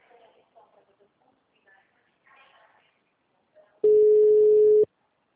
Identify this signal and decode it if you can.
Ringback tone (Europe, 425 Hz), with speech.